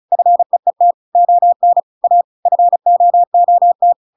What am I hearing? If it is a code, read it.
Morse code, heard as a keyed tone, about 34 wpm, 700 Hz. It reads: FEET ON A FOOT
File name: Pinocchio / Pinocchio ch6 35 35 ik1hge